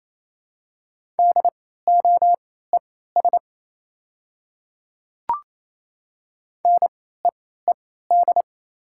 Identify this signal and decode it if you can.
Morse code, heard as a keyed tone, about 28 wpm, spaced as for 16 wpm, 700 Hz. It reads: DOES NEED
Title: 3x Character Spacing - Top 300 Words - 28wpm